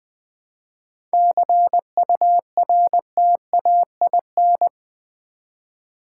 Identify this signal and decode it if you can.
Morse code, heard as a keyed tone, about 20 wpm, 700 Hz. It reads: CURTAIN
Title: 7 Letter Words - First 3 Letters Given - 20wpm